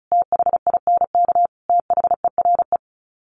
Morse code, heard as a keyed tone, about 34 wpm, 700 Hz. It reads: THINK THERE
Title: Sets of 4 Words 35wpm